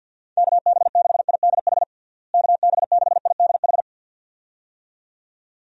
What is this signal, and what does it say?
Morse code, heard as a keyed tone, about 50 wpm, 700 Hz. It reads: KB6IDH KB6IDH
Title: USA Call Signs - 6 Characters - Repeated 2x - 50wpm